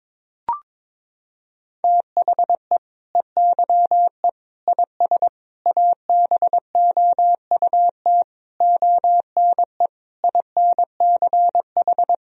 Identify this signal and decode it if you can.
Morse code, heard as a keyed tone, about 22 wpm, 700 Hz. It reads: THE EYE IS ABOUT ONE INCH
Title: Sentences from Top 1000 Words - Review Part 3 - 22wpm